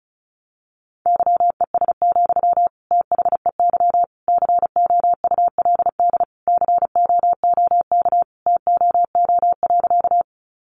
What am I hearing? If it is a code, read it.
Morse code, heard as a keyed tone, about 34 wpm, 700 Hz. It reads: YES, THEY COULD COOK TOO.
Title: Sentences from Top 700 Words 35wpm